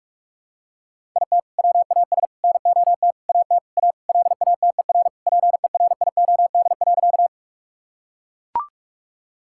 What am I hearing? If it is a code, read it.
Morse code, heard as a keyed tone, about 42 wpm, 700 Hz. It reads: IT WAS NOT AT A LATER PERIOD.
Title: Sentences from Top 1000 Words - Review Part 5 - 45wpm